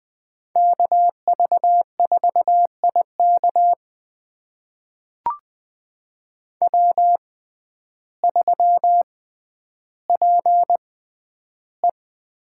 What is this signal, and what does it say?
Morse code, heard as a keyed tone, about 20 wpm, spaced as for 11 wpm, 700 Hz. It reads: KV4IK W3PE
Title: USA Call Signs - 5 Characters - 6x Character Spacing - 20wpm